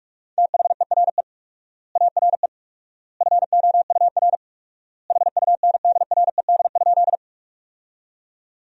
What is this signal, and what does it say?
Morse code, heard as a keyed tone, about 46 wpm, 700 Hz. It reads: THERE ARE FOUR HUNDRED?
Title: 4x Word-Spacing - Sentences from Top 400 Words - 45wpm